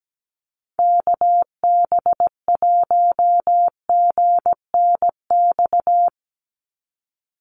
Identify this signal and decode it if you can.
Morse code, heard as a keyed tone, about 17 wpm, 700 Hz. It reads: KB1GNX
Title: USA Call Signs 17wpm